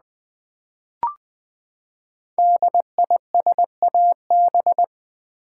Morse code, heard as a keyed tone, about 20 wpm, 700 Hz. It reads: DISAB
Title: Build A Word - 5 Letters At A Time - 10-Letter Words - 20wpm